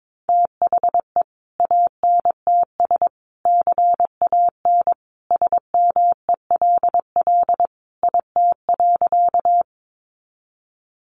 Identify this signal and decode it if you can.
Morse code, heard as a keyed tone, about 22 wpm, 700 Hz. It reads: THE ANTS CAN SMELL IT.
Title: Easy Phrases 22wpm